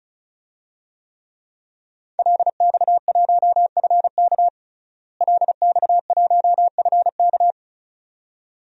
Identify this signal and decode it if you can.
Morse code, heard as a keyed tone, about 36 wpm, 700 Hz. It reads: LX1FK LX1FK
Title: DXCC-100 Call Signs - 5 Characters - Repeated 2x - 35wpm